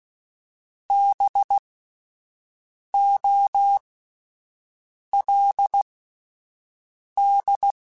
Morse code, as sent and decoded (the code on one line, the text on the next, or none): -... --- .-.. -..
BOLD